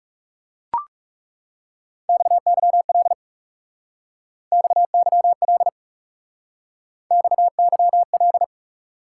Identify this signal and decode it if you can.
Morse code, heard as a keyed tone, about 40 wpm, 700 Hz. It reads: XYL XYL XYL